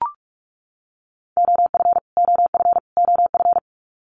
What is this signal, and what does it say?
Morse code, heard as a keyed tone, about 44 wpm, 700 Hz. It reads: OF OF OF
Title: Top 500 Words Repeated 45wpm